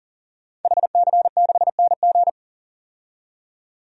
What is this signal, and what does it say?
Morse code, heard as a keyed tone, about 40 wpm, 700 Hz. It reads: HC6NG
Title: DXCC-100 Call Signs - 5 Characters - 40wpm